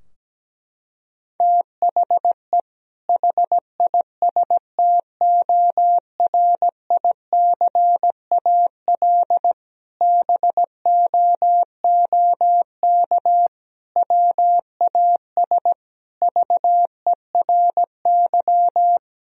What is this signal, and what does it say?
Morse code, heard as a keyed tone, about 17 wpm, 700 Hz. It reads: THE HISTORICAL BOOK WAS VERY
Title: Sentences from Top 1000 Words - Review Part 4 - 17wpm